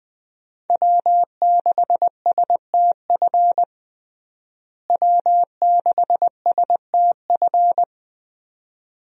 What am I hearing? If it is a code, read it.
Morse code, heard as a keyed tone, about 20 wpm, 700 Hz. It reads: W6STF W6STF
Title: USA Call Signs - 5 Characters - Repeated 2x - 20wpm